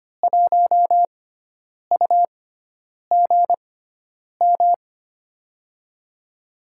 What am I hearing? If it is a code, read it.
Morse code, heard as a keyed tone, about 25 wpm, spaced as for 9 wpm, 700 Hz. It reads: 1UGM
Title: USA Call Signs - 6x Character Spacing - 25wpm